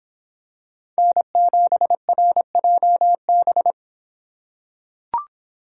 Morse code, heard as a keyed tone, about 26 wpm, 700 Hz. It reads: N7RJB